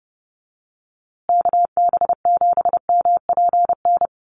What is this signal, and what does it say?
Morse code, heard as a keyed tone, about 30 wpm, 700 Hz. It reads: KB7MPN
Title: USA Call Signs Repeated 3x 30wpm